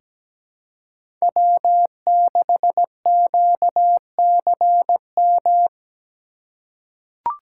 Morse code, heard as a keyed tone, about 17 wpm, 700 Hz. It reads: W6QCM
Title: USA Call Signs - 2x Character Spacing - 17wpm